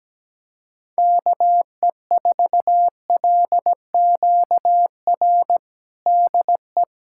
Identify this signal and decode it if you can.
Morse code, heard as a keyed tone, about 17 wpm, 700 Hz. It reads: KE4LQR DE
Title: USA Callsign DE Callsign - 6 Characters - 17wpm